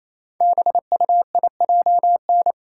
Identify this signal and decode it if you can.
Morse code, heard as a keyed tone, about 29 wpm, 700 Hz. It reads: BUIJN
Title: Code Groups - 5 Letters - 28wpm